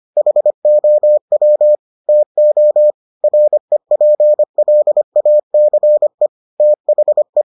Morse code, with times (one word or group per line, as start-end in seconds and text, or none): 0.2-1.8 HOW
2.1-2.9 TO
3.2-6.3 REPLACE
6.6-7.4 THE